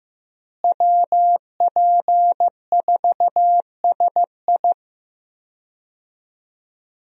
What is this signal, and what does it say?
Morse code, heard as a keyed tone, about 15 wpm, 700 Hz. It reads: WP4SI
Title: USA Call Signs - 5 Characters - Repeated 3x - 15wpm